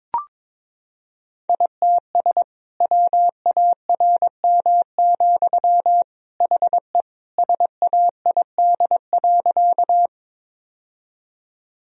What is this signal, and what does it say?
Morse code, heard as a keyed tone, about 22 wpm, 700 Hz. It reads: ITS WARM, HE SAID.
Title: Sentences from Top 1000 Words - Review Part 2 - 22wpm